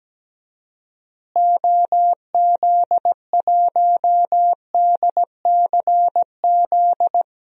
Morse code, heard as a keyed tone, about 17 wpm, 700 Hz. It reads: OZ1DCZ
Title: DXCC-100 Call Signs - 6 Characters - Repeated 2x - 17wpm